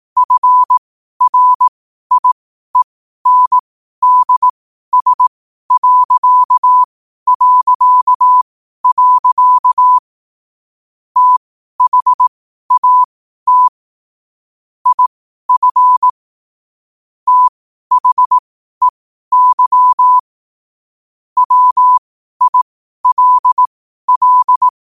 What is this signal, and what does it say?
Morse code, heard as a keyed tone, about 18 wpm, spaced as for 12 wpm, 1000 Hz. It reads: FRIENDS... THAT IF THEY WILL
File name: qotdfiles / 12wpm